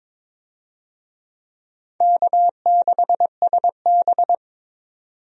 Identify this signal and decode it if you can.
Morse code, heard as a keyed tone, about 22 wpm, 700 Hz. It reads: K6SB
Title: USA Call Signs - 4 Characters - 22wpm